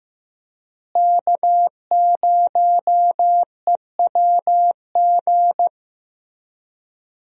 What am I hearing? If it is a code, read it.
Morse code, heard as a keyed tone, about 15 wpm, 700 Hz. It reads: K0EWG
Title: USA Call Signs - 5 Characters - Repeated 3x - 15wpm